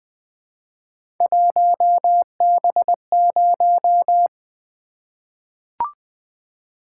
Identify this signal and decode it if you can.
Morse code, heard as a keyed tone, about 20 wpm, 700 Hz. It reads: 1B0
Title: Three Letter-Number Combinations - 20wpm